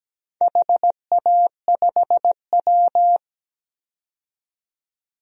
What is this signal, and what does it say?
Morse code, heard as a keyed tone, about 17 wpm, 700 Hz. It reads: HA5W
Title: DXCC-100 Call Signs - 4 Characters - 17wpm